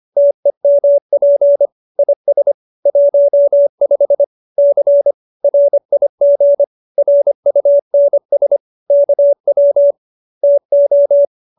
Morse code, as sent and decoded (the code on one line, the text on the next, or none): - . -- .--. / .. ... / .---- ..... / -.-. / .-. .. --. / .-. ..- -. ... / -.- .-- / - ---
TEMP IS 15 C RIG RUNS KW TO